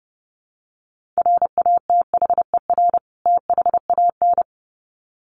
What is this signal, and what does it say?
Morse code, heard as a keyed tone, about 30 wpm, 700 Hz. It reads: RATHER THAN
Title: Sets of 2 Words 30wpm